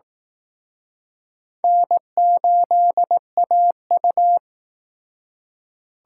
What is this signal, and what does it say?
Morse code, heard as a keyed tone, about 18 wpm, 700 Hz. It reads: N8AU
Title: Speed Racing - USA 1x2 Call Signs 15wpm